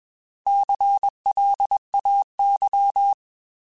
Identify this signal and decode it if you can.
Morse code, heard as a keyed tone, about 21 wpm, 800 Hz. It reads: CLAY
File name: Alphabetical / Alphabetical mp3s / Cs @20